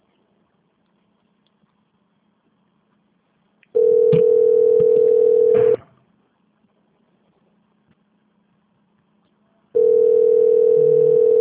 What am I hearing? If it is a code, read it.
Ringback tone (North America), 440 + 480 Hz.
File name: out- -2039- - - .1174